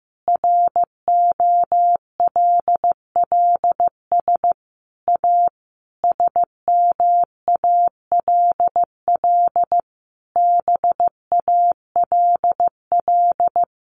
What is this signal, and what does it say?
Morse code, heard as a keyed tone, about 15 wpm, 700 Hz. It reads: ROLLS A SMALL BALL
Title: Easy Phrases 15wpm